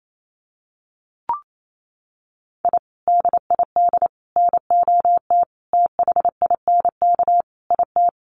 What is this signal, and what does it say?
Morse code, heard as a keyed tone, about 28 wpm, 700 Hz. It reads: I DID NOT THINK IT
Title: Sets of 5 Words Encore2 28wpm